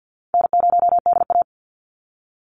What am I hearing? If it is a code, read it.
Morse code, heard as a keyed tone, about 50 wpm, 700 Hz. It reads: N0DA
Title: NA SOTA Chaser Callsigns 50wpm